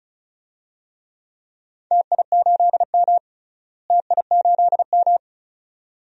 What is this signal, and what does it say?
Morse code, heard as a keyed tone, about 36 wpm, 700 Hz. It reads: TI8M TI8M
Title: DXCC-100 Call Signs - 4 Characters - Repeated 2x - 35wpm